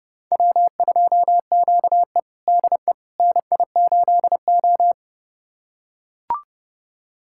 Morse code, heard as a keyed tone, about 31 wpm, 700 Hz. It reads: W2QE DE NI8O
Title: USA Callsign DE Callsign - 4 Characters - 30wpm